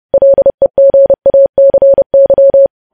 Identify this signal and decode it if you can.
Morse code, heard as a keyed tone, about 30 wpm, 550 Hz. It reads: LEGACY
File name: podcast / News headlines-1x-30x30